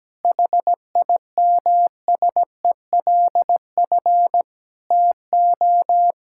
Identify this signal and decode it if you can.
Morse code, heard as a keyed tone, about 17 wpm, 700 Hz. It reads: HIMSELF TO
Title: Sentences from Top 1600 Words 17wpm